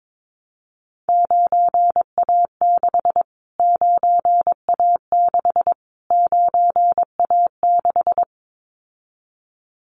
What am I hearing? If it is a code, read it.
Morse code, heard as a keyed tone, about 22 wpm, 700 Hz. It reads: 9A6 9A6 9A6
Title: Callsign Prefixes for Top 25 DXCC - Repeated 3x - 22wpm